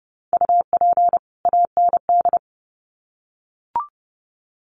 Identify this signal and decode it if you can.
Morse code, heard as a keyed tone, about 30 wpm, 700 Hz. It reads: UP AND
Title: Sets of 2 Words 30wpm